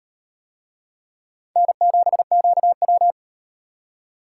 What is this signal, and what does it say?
Morse code, heard as a keyed tone, about 38 wpm, 700 Hz. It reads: N7QW